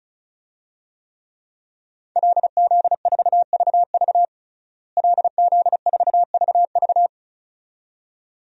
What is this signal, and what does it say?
Morse code, heard as a keyed tone, about 34 wpm, 700 Hz. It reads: LZ4VV LZ4VV
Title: DXCC-100 Call Signs - 5 Characters - Repeated 2x - 35wpm